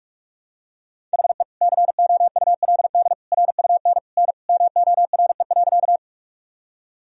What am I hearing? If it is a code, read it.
Morse code, heard as a keyed tone, about 46 wpm, 700 Hz. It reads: HE COULD RUN N MORE.
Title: Sentences from Top 300 Words and Abbreviations 45wpm